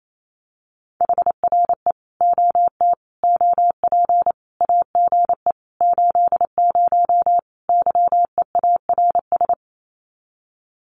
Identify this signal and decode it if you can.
Morse code, heard as a keyed tone, about 28 wpm, 700 Hz. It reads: HRE OT OP AGE 80 YEARS
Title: Common QSO Phrases 28wpm